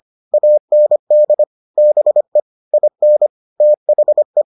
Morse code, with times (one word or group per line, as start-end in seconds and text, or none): 0.3-1.4 AND
1.8-2.4 BE
2.7-3.3 IN
3.6-4.4 THE